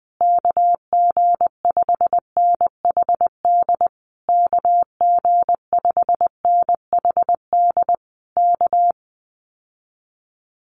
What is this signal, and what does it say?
Morse code, heard as a keyed tone, about 20 wpm, 700 Hz. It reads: KG5NHD KG5NHD K
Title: Common QSO Phrases Encore 20wpm